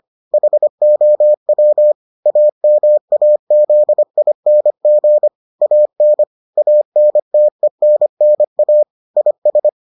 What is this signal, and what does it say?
Morse code, heard as a keyed tone, about 25 wpm, 600 Hz. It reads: HOW AMAZING AN ANTENNA IS